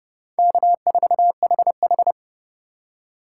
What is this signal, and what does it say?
Morse code, heard as a keyed tone, about 27 wpm, 700 Hz. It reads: K4HH